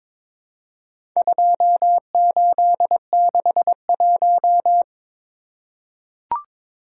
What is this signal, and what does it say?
Morse code, heard as a keyed tone, about 23 wpm, 700 Hz. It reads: 2861
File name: Four Letter-Number Combinations - 22wpm